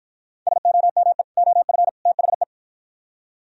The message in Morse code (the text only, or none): SOME OF THE